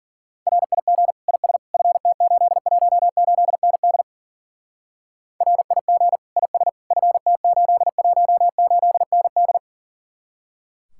Morse code, heard as a keyed tone, about 50 wpm, 700 Hz. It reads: RIG IS FT818ND RIG IS FT818ND